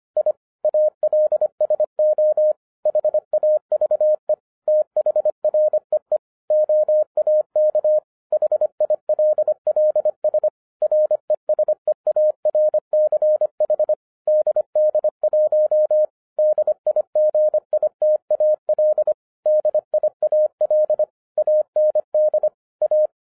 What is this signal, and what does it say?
Morse code, heard as a keyed tone, about 24 wpm, 610 Hz. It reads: I ALSO HAVE THREE OAK HILLS RESEARCH DD1 DIGITAL DIAL AND A
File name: 2011-07-mp3 / n24 QRP OTAKU 25wpm